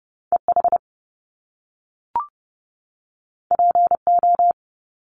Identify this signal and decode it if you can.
Morse code, heard as a keyed tone, about 30 wpm, 700 Hz. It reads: EH PO
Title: Top Two Letter Combinations in Words 30wpm